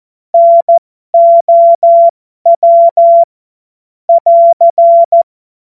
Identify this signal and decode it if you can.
Morse code, heard as a keyed tone, about 14 wpm, spaced as for 12 wpm, 680 Hz. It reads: NOW <AR>
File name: Morsekurs / Witze